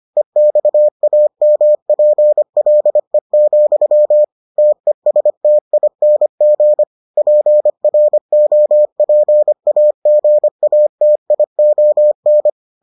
Morse code, heard as a keyed tone, about 25 wpm, 600 Hz. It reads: EXAMPLE, TESTING PROPAGATION